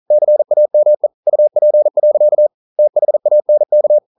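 Morse code, heard as a keyed tone, about 41 wpm, 600 Hz. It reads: CAME UP. THANK
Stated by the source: Morse code practice at 41 wpm, from CW med Gnister.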